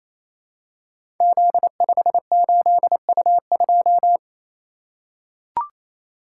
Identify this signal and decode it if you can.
Morse code, heard as a keyed tone, about 29 wpm, 700 Hz. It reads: Z58U2
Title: Code Groups - 5 Letter:Numbers - 28wpm